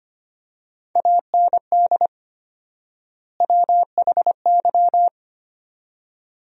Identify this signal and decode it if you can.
Morse code, heard as a keyed tone, about 25 wpm, 700 Hz. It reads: AND WHY
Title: 4x Word-Spacing - Sentences from Top 200 Words - 25wpm